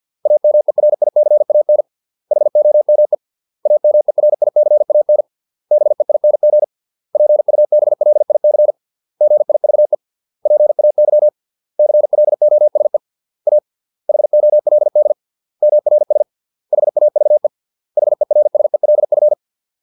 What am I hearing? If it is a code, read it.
Morse code, heard as a keyed tone, about 50 wpm, 600 Hz. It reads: AMERICAN HOME AMERICAN BEING PUBLIC GIVE PAY CLOSE A HOLD MRS HAVE HERSELF